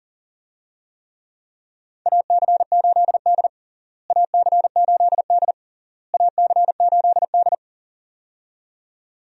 Morse code, heard as a keyed tone, about 40 wpm, 700 Hz. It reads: AC8D AC8D AC8D
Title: USA Call Signs - 4 Characters - Repeated 3x - 40wpm